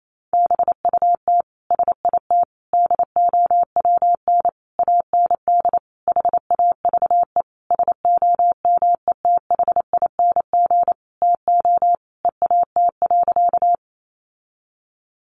Morse code, heard as a keyed tone, about 28 wpm, 700 Hz. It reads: BUT SIT DOWN AND HAVE SOMETHING TO EAT.
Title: Sentences from Top 600 Words 28wpm